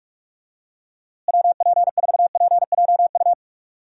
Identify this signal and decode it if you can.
Morse code, heard as a keyed tone, about 47 wpm, 700 Hz. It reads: WP4PJU
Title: USA Call Signs - 6 Characters - Repeated 2x - 45wpm